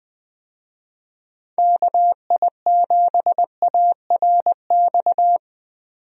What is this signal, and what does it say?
Morse code, heard as a keyed tone, about 20 wpm, 700 Hz. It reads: KI7ARX
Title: USA Call Signs - 6 Characters - 20wpm